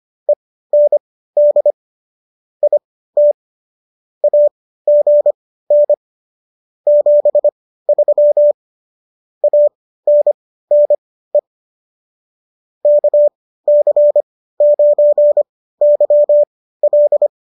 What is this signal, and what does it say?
Morse code, heard as a keyed tone, about 25 wpm, spaced as for 15 wpm, 590 Hz. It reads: END IT AGN 73 ANNE KC9YL